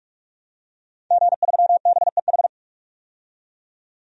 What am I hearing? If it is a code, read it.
Morse code, heard as a keyed tone, about 47 wpm, 700 Hz. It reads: G3BEH
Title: DXCC-100 Call Signs - 5 Characters - 45wpm